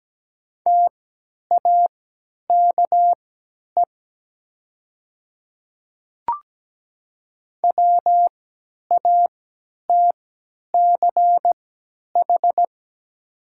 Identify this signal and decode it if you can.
Morse code, heard as a keyed tone, about 17 wpm, spaced as for 10 wpm, 700 Hz. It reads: TAKE WATCH
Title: 3x Character Spacing - Top 500 Words - 17wpm